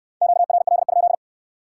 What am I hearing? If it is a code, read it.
Morse code, heard as a keyed tone, about 61 wpm, 700 Hz. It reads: QRL?